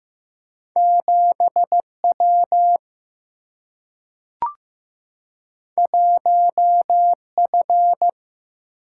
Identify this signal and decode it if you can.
Morse code, heard as a keyed tone, about 15 wpm, 700 Hz. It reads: 7W 1F
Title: Single-Letter Single-Number 15wpm